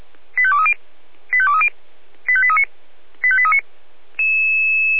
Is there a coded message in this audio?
Yes: five-tone selective call (ZVEI) from 0.4 to 3.6 s, heard as quick runs of five tones, tones of about 70 ms. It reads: 85230 85230 85839 85839